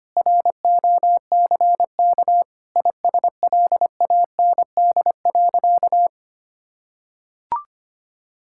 Morse code, heard as a keyed tone, about 25 wpm, 700 Hz. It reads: ROCK ISLAND.
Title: Sentences from Top 1600 Words 25wpm